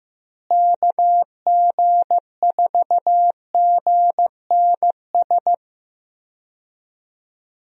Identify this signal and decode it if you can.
Morse code, heard as a keyed tone, about 15 wpm, 700 Hz. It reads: KG4GNS
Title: USA Call Signs - 6 Characters - Repeated 2x - 15wpm